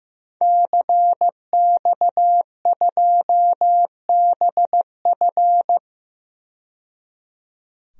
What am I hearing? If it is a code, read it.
Morse code, heard as a keyed tone, about 15 wpm, 700 Hz. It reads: CX2BF
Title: DXCC-100 Call Signs - 5 Characters - 15wpm